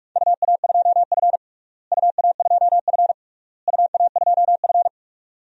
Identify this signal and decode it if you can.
Morse code, heard as a keyed tone, about 41 wpm, 700 Hz. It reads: UA2F UA2F UA2F
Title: DXCC-100 Call Signs - 4 Characters - Repeated 3x - 45wpm